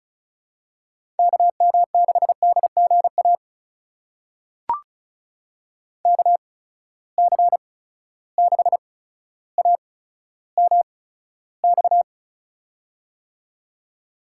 Morse code, heard as a keyed tone, about 36 wpm, spaced as for 15 wpm, 700 Hz. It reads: KM6DGA KC6AMX
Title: USA Call Signs - 6 Characters - 8x Character Spacing - 35wpm